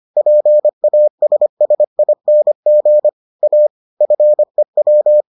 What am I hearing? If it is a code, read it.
Morse code, heard as a keyed tone, about 25 wpm, 600 Hz. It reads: PASSING A FEW